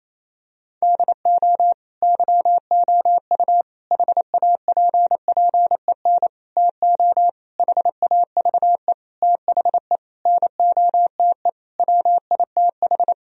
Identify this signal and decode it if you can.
Morse code, heard as a keyed tone, about 28 wpm, 700 Hz. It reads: DO YOU HAPPEN TO HAVE THE NOTE WITH
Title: Sentences from Top 1000 Words - Review Part 2 - 28wpm